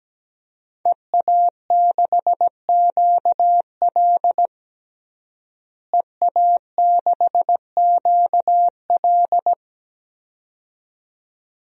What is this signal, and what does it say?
Morse code, heard as a keyed tone, about 17 wpm, 700 Hz. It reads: EA6QL EA6QL
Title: DXCC-100 Call Signs - 5 Characters - Repeated 2x - 17wpm